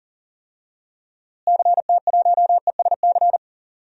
Morse code, heard as a keyed tone, about 40 wpm, 700 Hz. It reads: CT1ESC
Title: DXCC-100 Call Signs - 6 Characters - Repeated 2x - 40wpm